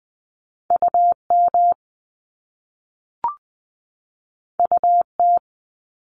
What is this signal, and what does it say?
Morse code, heard as a keyed tone, about 20 wpm, 700 Hz. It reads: UM UT
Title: Top Two Letter Combinations in Words 20wpm